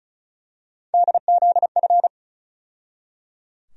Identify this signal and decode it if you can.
Morse code, heard as a keyed tone, about 34 wpm, 700 Hz. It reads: DZF
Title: USA Call Sign Postfixes - 3 Letters 35wpm